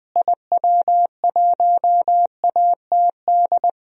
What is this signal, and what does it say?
Morse code, heard as a keyed tone, about 20 wpm, 700 Hz. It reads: IW1ATD
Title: DXCC-100 Call Signs - 6 Characters - Repeated 3x - 20wpm